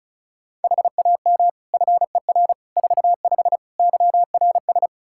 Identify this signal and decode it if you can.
Morse code, heard as a keyed tone, about 36 wpm, 700 Hz. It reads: HAM FER 45 YRS
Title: K1USN SST QSO Practice 35wpm